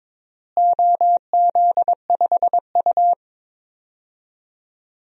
Morse code, heard as a keyed tone, about 22 wpm, 700 Hz. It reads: OZ5U
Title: DXCC-100 Call Signs - 4 Characters - Repeated 3x - 22wpm